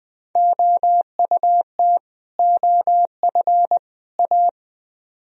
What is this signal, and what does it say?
Morse code, heard as a keyed tone, about 21 wpm, 700 Hz. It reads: OUT OF A
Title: Speed-Racing - Sets of 3 Words 17wpm